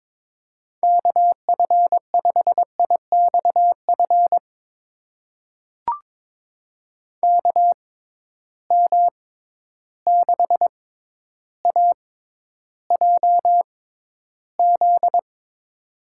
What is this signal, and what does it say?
Morse code, heard as a keyed tone, about 22 wpm, spaced as for 11 wpm, 700 Hz. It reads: KF5IXF KM6AJZ
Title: USA Call Signs - 6x Character Spacing - 22wpm